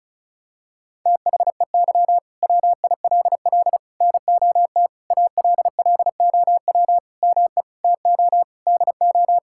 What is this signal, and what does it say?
Morse code, heard as a keyed tone, about 34 wpm, 700 Hz. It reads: THEY WILL NOT ALLOW ME TO DO
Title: Sentences from Top 1000 Words - Review Part 5 - 35wpm